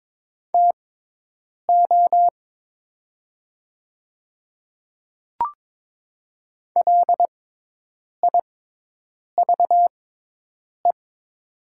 Morse code, heard as a keyed tone, about 22 wpm, spaced as for 8 wpm, 700 Hz. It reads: TO LIVE
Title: 6x Character Spacing - Top 500 Words - 22wpm